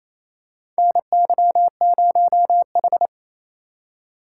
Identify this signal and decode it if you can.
Morse code, heard as a keyed tone, about 27 wpm, 700 Hz. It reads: NY0H
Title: USA Call Signs - 4 Characters - 2x Character Spacing - 28wpm